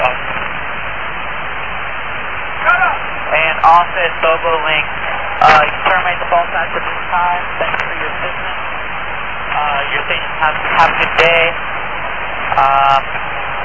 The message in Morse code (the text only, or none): none